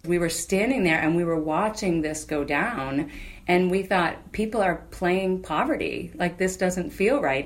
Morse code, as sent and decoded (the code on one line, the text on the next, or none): none